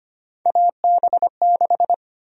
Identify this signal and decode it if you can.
Morse code, heard as a keyed tone, about 25 wpm, 700 Hz. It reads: AB6
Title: USA Call Sign Prefixes - 25wpm